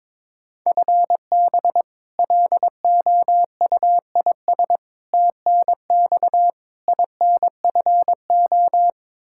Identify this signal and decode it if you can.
Morse code, heard as a keyed tone, about 22 wpm, 700 Hz. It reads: FB LOUIS TNX INFO
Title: Practice QSO - Exchange 3 - More Info - 22wpm